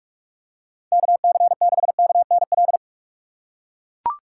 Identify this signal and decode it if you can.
Morse code, heard as a keyed tone, about 42 wpm, 700 Hz. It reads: KC6KNL
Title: USA Call Signs - 6 Characters - 6x Character Spacing - 45wpm